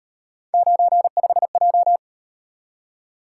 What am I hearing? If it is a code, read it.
Morse code, heard as a keyed tone, about 38 wpm, 700 Hz. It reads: 95J